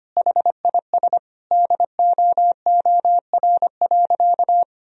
Morse code, heard as a keyed tone, about 25 wpm, 700 Hz. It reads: HIS DOOR.
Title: Sentences from Top 1000 Words - Review Part 2 - 25wpm